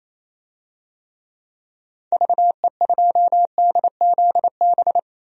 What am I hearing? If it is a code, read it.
Morse code, heard as a keyed tone, about 28 wpm, 700 Hz. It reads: VE2DZB
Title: DXCC-100 Call Signs - 6 Characters - 28wpm